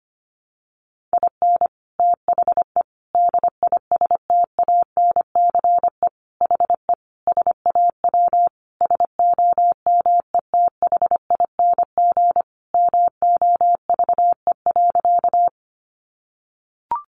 Morse code, heard as a keyed tone, about 25 wpm, 700 Hz. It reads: IN THE DISTANCE HE SAW SOMETHING MOVE.
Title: Sentences from Top 1200 Words 25wpm